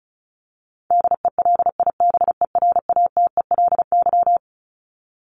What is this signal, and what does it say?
Morse code, heard as a keyed tone, about 34 wpm, 700 Hz. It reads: DELIBERATELY
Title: Top 1751-2000 Words 35wpm